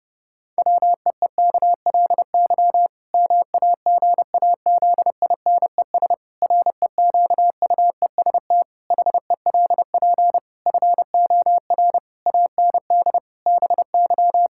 Morse code, heard as a keyed tone, about 30 wpm, 700 Hz. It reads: WEEKLY MAGAZINES REQUEST HELP FOR AND BY
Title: Pangram Challenge v2 - 30wpm